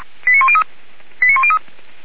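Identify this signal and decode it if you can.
Five-tone selective call (ZVEI), heard as quick runs of five tones, tones of about 70 ms. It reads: 89103 89103
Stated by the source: ZVEI five-tone selective call received by SDR.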